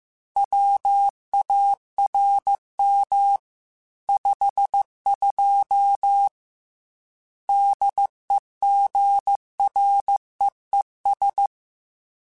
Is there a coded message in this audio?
Yes: Morse code from 0.4 to 11.5 s, heard as a keyed tone, about 15 wpm, 800 Hz. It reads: WARM 52 DEGREES